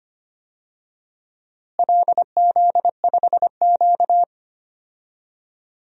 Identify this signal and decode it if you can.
Morse code, heard as a keyed tone, about 25 wpm, 700 Hz. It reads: LZ5Q
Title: DXCC-100 Call Signs - 4 Characters - Repeated 2x - 25wpm